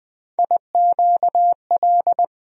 Morse code, heard as a keyed tone, about 20 wpm, 700 Hz. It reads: IQL
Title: USA Call Sign Postfixes - 3 Letters 20wpm